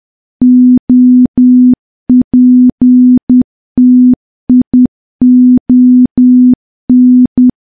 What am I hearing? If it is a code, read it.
Morse code, heard as a keyed tone, about 10 wpm, 250 Hz. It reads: OPTION
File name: morse / morse-training-10WPM